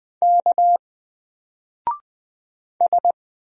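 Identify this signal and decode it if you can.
Morse code, heard as a keyed tone, about 20 wpm, 700 Hz. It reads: K S
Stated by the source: Single letter and number Morse drill at 20 wpm, from N4LCD.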